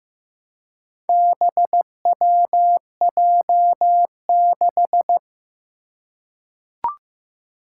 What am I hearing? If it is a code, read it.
Morse code, heard as a keyed tone, about 15 wpm, 700 Hz. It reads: BWJ6
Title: Four Challenging Letter-Number Combinations 15wpm